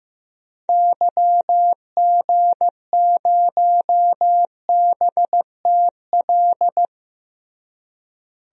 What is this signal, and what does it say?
Morse code, heard as a keyed tone, about 15 wpm, 700 Hz. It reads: YG0BTL
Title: DXCC-100 Call Signs - 6 Characters - Repeated 2x - 15wpm